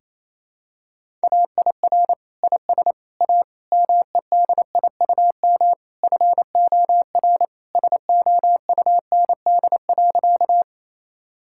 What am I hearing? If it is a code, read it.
Morse code, heard as a keyed tone, about 28 wpm, 700 Hz. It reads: AIR IS A MEDIUM FOR SOUND.
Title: Sentences from Top 1000 Words - Review Part 6 - 28wpm